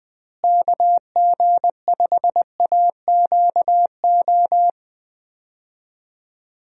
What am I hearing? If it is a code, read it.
Morse code, heard as a keyed tone, about 21 wpm, 700 Hz. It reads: KG5AQO
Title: USA Call Signs - 6 Characters - Repeated 3x - 20wpm